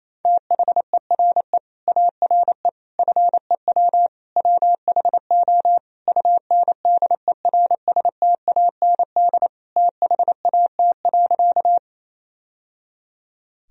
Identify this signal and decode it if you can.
Morse code, heard as a keyed tone, about 28 wpm, 700 Hz. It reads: THERE ARE FEW WHO UNDERSTAND THAT.
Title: Sentences from Top 1000 Words - Review Part 1 - 28wpm